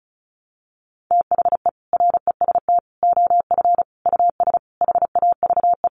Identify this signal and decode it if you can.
Morse code, heard as a keyed tone, about 34 wpm, 700 Hz. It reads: THE REST OF US HAVE
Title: Sentences from Top 700 Words 35wpm